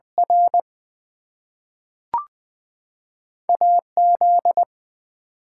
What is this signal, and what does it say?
Morse code, heard as a keyed tone, about 20 wpm, 700 Hz. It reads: R AZ
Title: US States and Canadian Provinces 20wpm